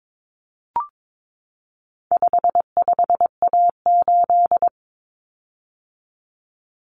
Morse code, heard as a keyed tone, about 22 wpm, 700 Hz. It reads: 55A8
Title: Common Cut Numbers - 4-Digit 22wpm